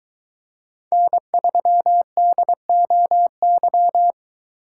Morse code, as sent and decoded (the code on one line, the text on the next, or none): -. ...-- -.. --- -.--
N3DOY